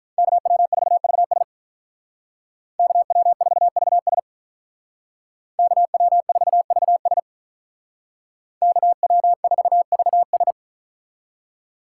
Morse code, heard as a keyed tone, about 43 wpm, 700 Hz. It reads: KW4VS KW4VS KW4VS KW4VS
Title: Speed Racing - USA 2x2 Call Signs 35wpm